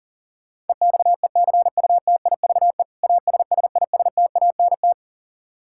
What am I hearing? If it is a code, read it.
Morse code, heard as a keyed tone, about 40 wpm, 700 Hz. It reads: EXECUTIVE ASSISTANT
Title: Practice QSO - Exchange 3 - More Info - 40wpm